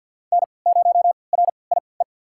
Morse code, heard as a keyed tone, about 50 wpm, spaced as for 28 wpm, 700 Hz. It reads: N0RIE
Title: USA Call Signs - 5 Characters - 3x Character Spacing - 50wpm